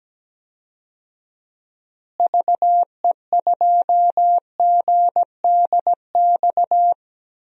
Morse code, heard as a keyed tone, about 17 wpm, 700 Hz. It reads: VE2GDX
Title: DXCC-100 Call Signs - 6 Characters - Repeated 3x - 17wpm